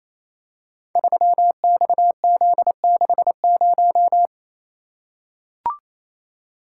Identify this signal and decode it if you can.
Morse code, heard as a keyed tone, about 28 wpm, 700 Hz. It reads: 3XZ60
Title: Code Groups - 5 Letter:Numbers - 28wpm